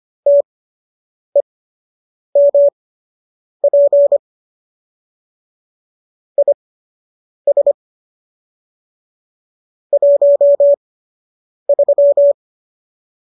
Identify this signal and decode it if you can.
Morse code, heard as a keyed tone, about 25 wpm, spaced as for 8 wpm, 570 Hz. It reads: TEMP IS 13